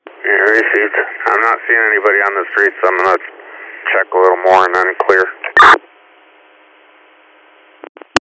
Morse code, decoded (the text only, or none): none